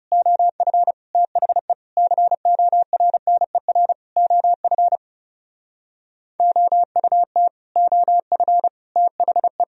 Morse code, heard as a keyed tone, about 38 wpm, 700 Hz. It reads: OF THE CORNER OF OUT OF THE